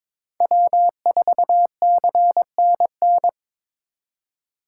Morse code, heard as a keyed tone, about 22 wpm, 700 Hz. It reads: W4CNN
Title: USA Call Signs - 4x Character Spacing - 22wpm